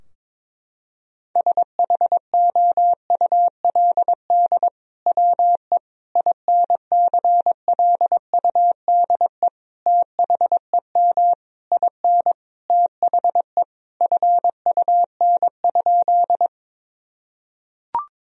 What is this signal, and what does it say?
Morse code, heard as a keyed tone, about 22 wpm, 700 Hz. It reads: SHOULD WE INCLUDE THEM IN THE FUN?
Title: Sentences from Top 1000 Words - Review Part 3 - 22wpm